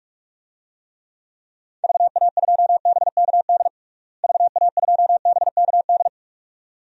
Morse code, heard as a keyed tone, about 46 wpm, 700 Hz. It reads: VA2BKD VA2BKD
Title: DXCC-100 Call Signs - 6 Characters - Repeated 2x - 45wpm